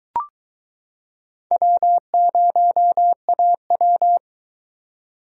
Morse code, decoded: W0AW

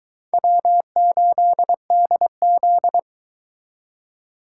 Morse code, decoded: W8DZ

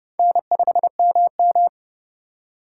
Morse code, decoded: N5MM

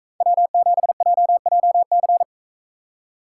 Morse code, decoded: W7JJC

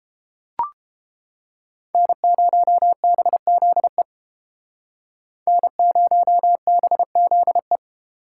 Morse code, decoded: N0BZE N0BZE

